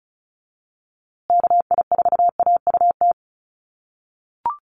Morse code, heard as a keyed tone, about 34 wpm, 700 Hz. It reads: KI4AUT
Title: USA Call Signs Repeated 2x 35wpm